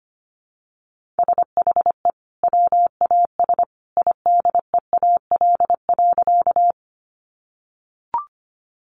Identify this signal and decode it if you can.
Morse code, heard as a keyed tone, about 25 wpm, 700 Hz. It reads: SHE WAS IDEAL.
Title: Sentences from Top 1400 Words 25wpm